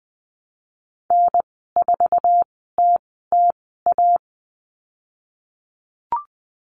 Morse code, heard as a keyed tone, about 20 wpm, spaced as for 15 wpm, 700 Hz. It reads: N4TTA
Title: 088 Lesson 15g - Number 5 - Call Signs 20wpm